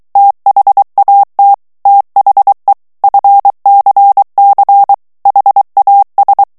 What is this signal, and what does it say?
Morse code, heard as a keyed tone, about 23 wpm, 800 Hz. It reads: THAT THE FCC HAS
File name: ANGLAIS texte clair 3 vitesse 22 mots minute